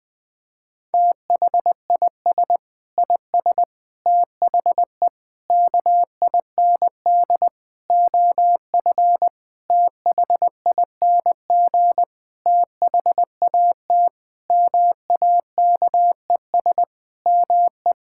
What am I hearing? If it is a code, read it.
Morse code, heard as a keyed tone, about 20 wpm, 700 Hz. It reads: THIS IS THE KIND OF THING THAT MAKES ME
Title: Sentences with 7-Word Phrases - Spoken Beginning - 20wpm